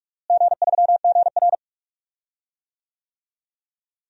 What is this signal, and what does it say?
Morse code, heard as a keyed tone, about 46 wpm, 700 Hz. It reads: G3GR
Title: DXCC-100 Call Signs - 4 Characters - Repeated 2x - 45wpm